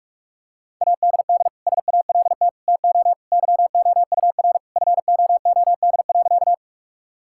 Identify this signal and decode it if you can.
Morse code, heard as a keyed tone, about 42 wpm, 700 Hz. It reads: ADD SALT TO YOUR FOOD.
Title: Sentences from Top 1000 Words - Review Part 5 - 45wpm